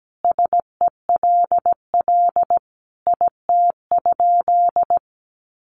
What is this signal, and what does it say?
Morse code, heard as a keyed tone, about 17 wpm, 700 Hz. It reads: SELL IT?
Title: Sentences from Top 1100 Words 17wpm